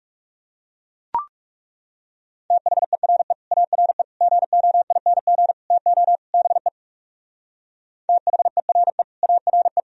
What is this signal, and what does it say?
Morse code, heard as a keyed tone, about 37 wpm, 700 Hz. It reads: THERE ARE GOING TO BE THERE ARE
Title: Speed-Racing - Sets of 5 Words 30wpm